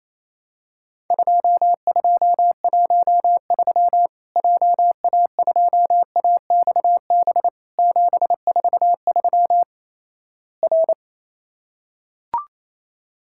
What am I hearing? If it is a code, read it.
Morse code, heard as a keyed tone, about 28 wpm, 700 Hz. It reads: 2213 JA2AXB 743 R
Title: Worked All Europe DX Contest - QTC Exchange - 28wpm